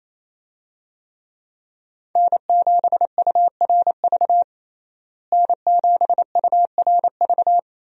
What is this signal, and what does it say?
Morse code, heard as a keyed tone, about 28 wpm, 700 Hz. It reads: N7URV N7URV
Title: USA Call Signs - 5 Characters - Repeated 3x - 28wpm